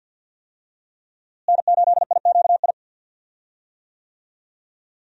Morse code, heard as a keyed tone, about 46 wpm, 700 Hz. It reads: N8IQI
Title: USA Call Signs - 5 Characters - Repeated 2x - 50wpm